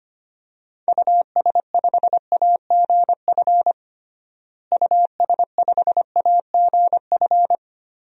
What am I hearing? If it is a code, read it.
Morse code, heard as a keyed tone, about 25 wpm, 700 Hz. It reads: US5AGF US5AGF
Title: DXCC-100 Call Signs - 6 Characters - Repeated 2x - 25wpm